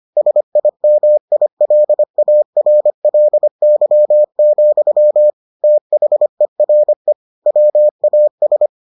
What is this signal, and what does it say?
Morse code, heard as a keyed tone, about 25 wpm, 600 Hz. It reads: SIMILARLY, THERE WAS